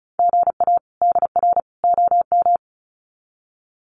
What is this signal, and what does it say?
Morse code, heard as a keyed tone, about 34 wpm, 700 Hz. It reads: GA DR OM
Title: Common QSO Phrases 35wpm